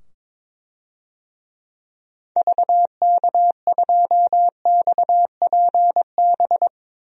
Morse code, heard as a keyed tone, about 22 wpm, 700 Hz. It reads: VK2XPB
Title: DXCC-100 Call Signs - 6 Characters - 22wpm